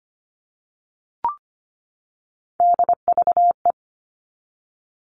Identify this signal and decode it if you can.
Morse code, heard as a keyed tone, about 25 wpm, 700 Hz. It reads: DVE
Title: All Cut Numbers - 3-Digit 25wpm